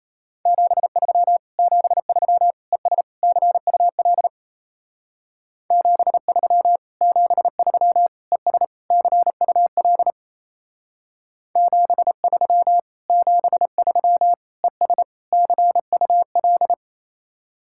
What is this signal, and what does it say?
Morse code, heard as a keyed tone, about 35 wpm, 700 Hz. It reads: 73 73 ES CUL 73 73 ES CUL 73 73 ES CUL
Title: Speed-Racing Common QSO Phrases Encore - 28wpm